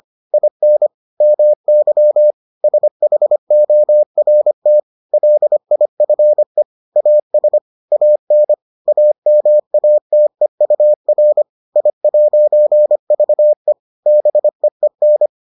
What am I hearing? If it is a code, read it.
Morse code, heard as a keyed tone, about 25 wpm, 600 Hz. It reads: IN MY SHORT LIFE AS AN AMATEUR I'VE BEEN